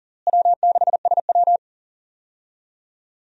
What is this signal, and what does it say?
Morse code, heard as a keyed tone, about 40 wpm, 700 Hz. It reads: W6SW
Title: USA Call Signs - 4 Characters - Repeated 2x - 40wpm